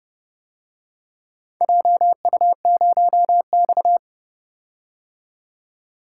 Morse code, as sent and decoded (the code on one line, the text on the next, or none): .--- ..- ----- -..-
JU0X